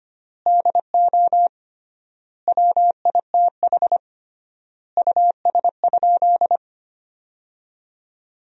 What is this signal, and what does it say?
Morse code, heard as a keyed tone, about 25 wpm, 700 Hz. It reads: DO WITH US?
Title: 3x Word-Spacing - Sentences from Top 200 Words - 25wpm